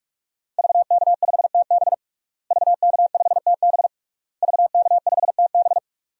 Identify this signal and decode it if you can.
Morse code, heard as a keyed tone, about 42 wpm, 700 Hz. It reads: VK5TB VK5TB VK5TB
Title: DXCC-100 Call Signs - 5 Characters - Repeated 3x - 45wpm